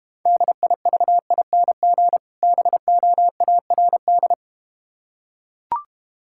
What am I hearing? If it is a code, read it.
Morse code, heard as a keyed tone, about 32 wpm, 700 Hz. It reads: DIVING BOARD